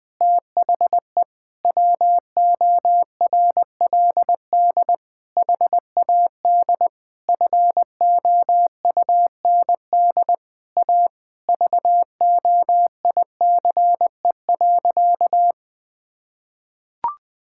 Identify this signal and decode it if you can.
Morse code, heard as a keyed tone, about 20 wpm, 700 Hz. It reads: THE WORLD HAD FOUND A VOICE.
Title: Sentences from Top 1600 Words 20wpm